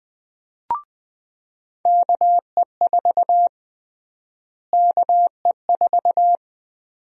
Morse code, heard as a keyed tone, about 21 wpm, 700 Hz. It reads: KE4 KE4
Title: USA Call Sign Prefixes - Repeated 2x - 20wpm